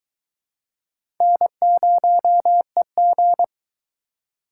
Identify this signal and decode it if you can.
Morse code, heard as a keyed tone, about 22 wpm, 700 Hz. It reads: N0EG